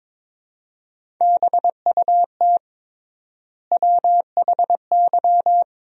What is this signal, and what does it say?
Morse code, heard as a keyed tone, about 22 wpm, 700 Hz. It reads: BUT WHY
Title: 3x Word-Spacing - Sentences from Top 200 Words - 22wpm